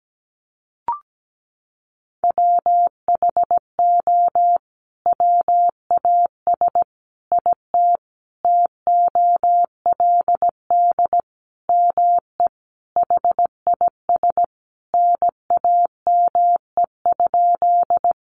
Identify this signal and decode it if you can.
Morse code, heard as a keyed tone, about 17 wpm, 700 Hz. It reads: WHO WAS IT TOLD ME HIS NAME?
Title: Sentences from Top 400 Words 17wpm